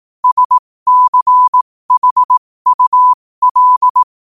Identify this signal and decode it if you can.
Morse code, heard as a keyed tone, about 18 wpm, spaced as for 15 wpm, 1000 Hz. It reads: SCHUL
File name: qotdfiles / 15wpm